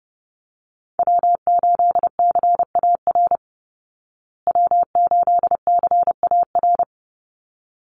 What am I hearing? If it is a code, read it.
Morse code, heard as a keyed tone, about 30 wpm, 700 Hz. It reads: W8CAR W8CAR
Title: Call Signs Repeated 3x - 30wpm